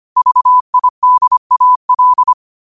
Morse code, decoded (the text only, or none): VIDAL